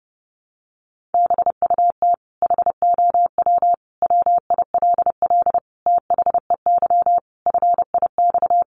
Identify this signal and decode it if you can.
Morse code, heard as a keyed tone, about 30 wpm, 700 Hz. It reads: BUT HOW WILL THEY FIX
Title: Sentences from Top 1400 Words 30wpm